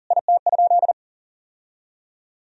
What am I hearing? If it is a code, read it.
Morse code, heard as a keyed tone, about 40 wpm, 700 Hz. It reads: IT?
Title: Sentences from Top 1000 Words - Review Part 4 - 40wpm